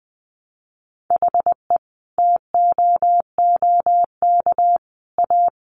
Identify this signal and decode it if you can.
Morse code, heard as a keyed tone, about 20 wpm, 700 Hz. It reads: HE TOOK A
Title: Sentences from Top 1000 Words 20wpm